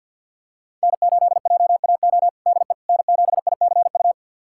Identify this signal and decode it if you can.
Morse code, heard as a keyed tone, about 46 wpm, 700 Hz. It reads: N8JAO DE N7ICU
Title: USA Callsign DE Callsign 50wpm